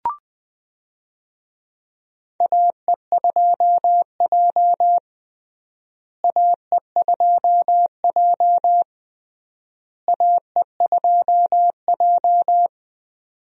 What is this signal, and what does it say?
Morse code, heard as a keyed tone, about 20 wpm, 700 Hz. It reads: AE2J AE2J AE2J
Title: USA Call Signs - 4 Characters - Repeated 3x - 20wpm